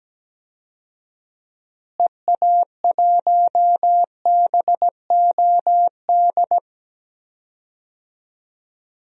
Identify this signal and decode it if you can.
Morse code, heard as a keyed tone, about 17 wpm, 700 Hz. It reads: EA1BOD
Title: DXCC-100 Call Signs - 6 Characters - Repeated 2x - 17wpm